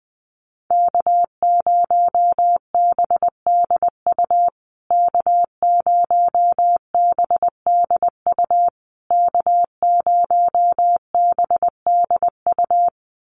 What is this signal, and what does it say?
Morse code, heard as a keyed tone, about 20 wpm, 700 Hz. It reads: K0BDU K0BDU K0BDU
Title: USA Call Signs Repeated 3x 20wpm